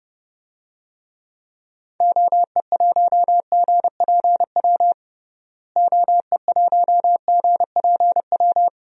Morse code, heard as a keyed tone, about 32 wpm, 700 Hz. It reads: OE1GPW OE1GPW
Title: DXCC-100 Call Signs - 6 Characters - Repeated 3x - 30wpm